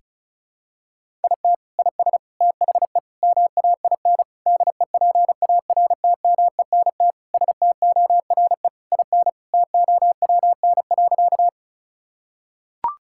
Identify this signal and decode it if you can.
Morse code, heard as a keyed tone, about 36 wpm, 700 Hz. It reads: IT IS THE MAIN DEPARTMENT STORE IN TOWN.
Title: Sentences from Top 1000 Words - Review Part 5 - 35wpm